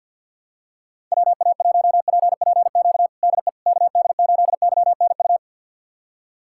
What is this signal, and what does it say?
Morse code, heard as a keyed tone, about 47 wpm, 690 Hz. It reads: WA1PPQ DE KD8YNU
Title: USA Callsign DE Callsign - 6 Characters - 50wpm